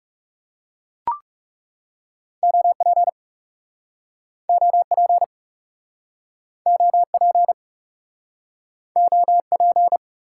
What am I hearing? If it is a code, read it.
Morse code, heard as a keyed tone, about 40 wpm, 700 Hz. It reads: OP OP OP OP